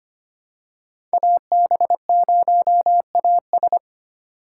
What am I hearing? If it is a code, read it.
Morse code, heard as a keyed tone, about 25 wpm, 700 Hz. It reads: AB0AS
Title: USA Call Signs - 3x Character Spacing - 25wpm